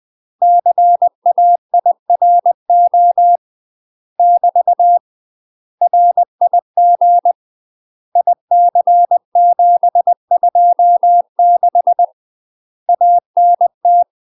Morse code, heard as a keyed tone, about 20 wpm, 700 Hz. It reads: CAIRO <BT> RIG IC726 ANT
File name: UK MASTER QSOs TRAINING / 20WPM / UK MASTER QSOs 0042